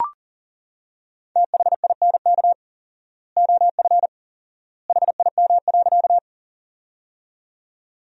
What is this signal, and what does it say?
Morse code, heard as a keyed tone, about 38 wpm, 700 Hz. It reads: THINK OF HIM.
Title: 4x Word-Spacing - Sentences from Top 200 Words - 40wpm